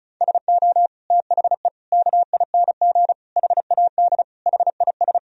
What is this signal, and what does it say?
Morse code, heard as a keyed tone, about 34 wpm, 700 Hz. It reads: SO THE KING HAD HIS
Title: Sentences from Top 1000 Words - Review Part 1 - 35wpm